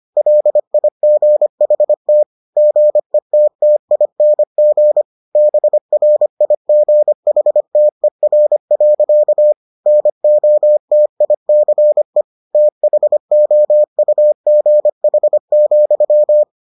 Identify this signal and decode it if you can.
Morse code, heard as a keyed tone, about 25 wpm, 600 Hz. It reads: LIGHT GETTING BRIGHTER. NOTICE THOUGH,